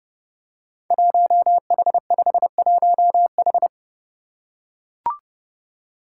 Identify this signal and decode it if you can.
Morse code, heard as a keyed tone, about 30 wpm, 700 Hz. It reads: 1H51H
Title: Five Challenging Letter-Number Combinations 30wpm